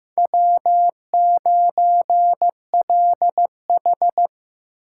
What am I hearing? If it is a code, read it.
Morse code, heard as a keyed tone, about 15 wpm, 700 Hz. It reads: W9LH
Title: USA Call Signs - 4 Characters - 3x Character Spacing - 15wpm